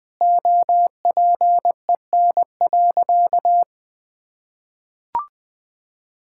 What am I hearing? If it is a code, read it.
Morse code, heard as a keyed tone, about 20 wpm, 700 Hz. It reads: OPEN.
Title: Sentences from Top 1600 Words 20wpm